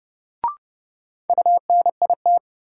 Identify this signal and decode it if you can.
Morse code, heard as a keyed tone, about 31 wpm, 700 Hz. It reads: UNIT
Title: Sentences from Top 1000 Words - Review Part 2 - New Word Drill - 30wpm